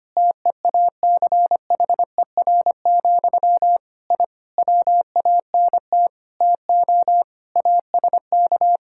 Morse code, heard as a keyed tone, about 25 wpm, 700 Hz. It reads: TEACHER, I WANT TO ASK
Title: Sentences from Top 1000 Words - Review Part 6 - 25wpm